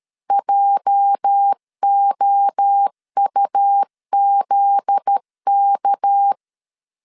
Morse code, heard as a keyed tone, about 13 wpm, 800 Hz. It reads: JOUZK
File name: CW / DET / 1 9 1500C S C 12wpm 3811